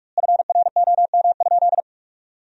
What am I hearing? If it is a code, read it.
Morse code, heard as a keyed tone, about 42 wpm, 700 Hz. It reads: FROM?
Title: 3x Word-Spacing - Sentences from Top 400 Words - 45wpm